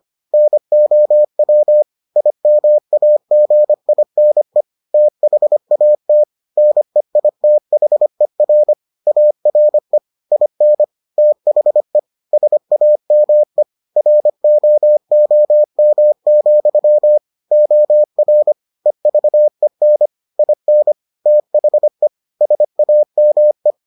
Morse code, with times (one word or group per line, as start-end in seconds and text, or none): 0.3-1.8 NOW
2.2-4.6 IMAGINE
4.9-6.2 THAT
6.6-8.7 NEITHER
9.1-10.0 ARE
10.3-10.8 IN
11.2-12.0 THE
12.3-13.6 SAME
14.0-17.2 ROOM,
17.5-18.5 OR
18.9-20.1 EVEN
20.4-20.9 IN
21.3-22.1 THE
22.4-23.7 SAME